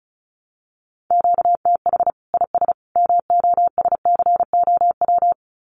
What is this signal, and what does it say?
Morse code, heard as a keyed tone, about 34 wpm, 700 Hz. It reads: QTH IS MOSCOW
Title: Common QSO Phrases 35wpm